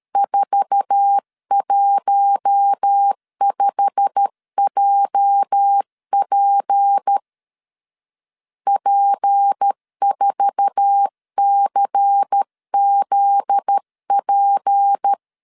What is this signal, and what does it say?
Morse code, heard as a keyed tone, about 13 wpm, 800 Hz. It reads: 415JP P4CZP
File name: LEZIONI / 8 YJ45 / 8 YJ45 12wpm 2151